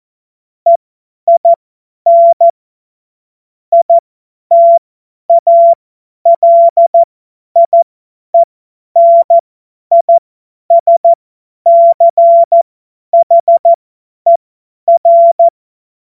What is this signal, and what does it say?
Morse code, heard as a keyed tone, about 14 wpm, spaced as for 10 wpm, 680 Hz. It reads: EIN ITALIENISCHER
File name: Morsekurs / Witze / Witz 19. April 2022 50 F70